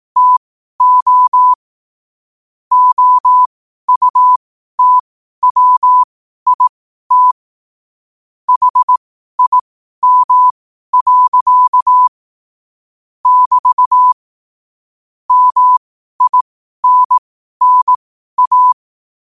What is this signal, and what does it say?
Morse code, heard as a keyed tone, about 18 wpm, spaced as for 12 wpm, 1000 Hz. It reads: TO OUTWIT HIM. <BT> MINNA
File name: qotdfiles / 12wpm